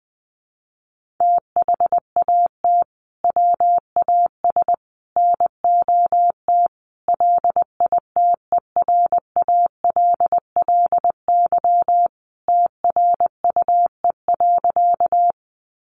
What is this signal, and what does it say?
Morse code, heard as a keyed tone, about 20 wpm, 700 Hz. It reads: THAT WAS NOT LITERALLY TRUE.
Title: Sentences from Top 1400 Words 20wpm